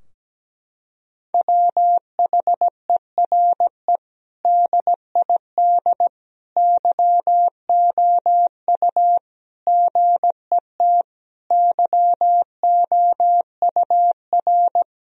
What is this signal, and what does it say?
Morse code, heard as a keyed tone, about 17 wpm, 700 Hz. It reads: WHERE DID YOU GET YOUR
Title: Sentences from Top 1000 Words - Review Part 4 - 17wpm